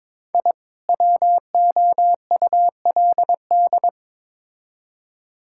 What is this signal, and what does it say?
Morse code, heard as a keyed tone, about 22 wpm, 700 Hz. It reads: I WOULD
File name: Build A Phrase - 2-Word Phrases - 22wpm